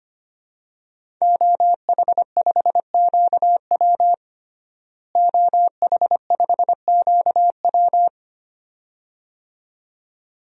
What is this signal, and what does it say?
Morse code, heard as a keyed tone, about 25 wpm, 700 Hz. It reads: OH5QW OH5QW
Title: DXCC-100 Call Signs - 5 Characters - Repeated 2x - 25wpm